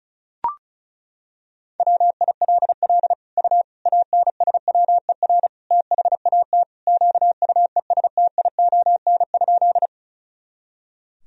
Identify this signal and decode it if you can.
Morse code, heard as a keyed tone, about 36 wpm, 700 Hz. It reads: WILL U ANSWER THAT QUESTION?
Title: Sentences from Top 400 Words and Abbreviations 35wpm